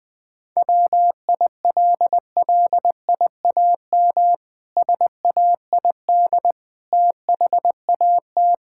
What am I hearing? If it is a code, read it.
Morse code, heard as a keyed tone, about 20 wpm, 700 Hz. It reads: WILLIAM SAID THAT
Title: Pangram Challenge v2 - 20wpm